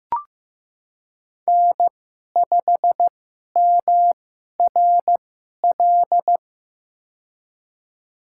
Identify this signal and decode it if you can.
Morse code, heard as a keyed tone, about 15 wpm, spaced as for 11 wpm, 700 Hz. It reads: N5MRL
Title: USA Call Signs - 5 Characters - 2x Character Spacing - 15wpm